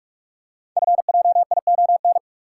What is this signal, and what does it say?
Morse code, heard as a keyed tone, about 44 wpm, 690 Hz. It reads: FJION
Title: Code Groups - 5 Letters - 45wpm